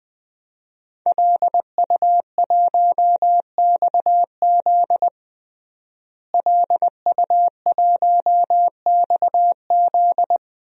Morse code, heard as a keyed tone, about 20 wpm, 700 Hz. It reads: LU1XZ LU1XZ
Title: DXCC-100 Call Signs - 5 Characters - Repeated 2x - 20wpm